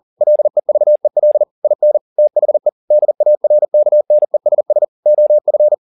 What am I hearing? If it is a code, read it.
Morse code, heard as a keyed tone, about 40 wpm, 600 Hz. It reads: LEVEL IN THE DARKNESS OF